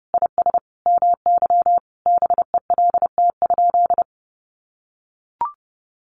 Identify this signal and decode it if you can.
Morse code, heard as a keyed tone, about 30 wpm, 700 Hz. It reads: IS MY BELT?
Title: Sentences from Top 1400 Words 30wpm